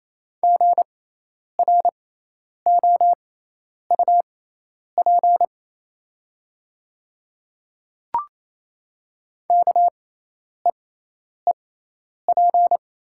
Morse code, heard as a keyed tone, about 28 wpm, spaced as for 10 wpm, 700 Hz. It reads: GROUP KEEP
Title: 6x Character Spacing - Top 300 Words - 28wpm